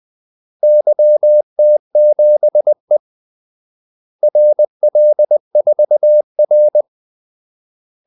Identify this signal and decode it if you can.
Morse code, heard as a keyed tone, about 20 wpm, 600 Hz. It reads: YT7E RL4R
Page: Kaldesignaler 20 wpm